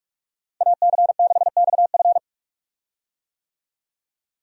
Morse code, heard as a keyed tone, about 46 wpm, 700 Hz. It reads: AC6XF
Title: USA Call Signs - 5 Characters - 45wpm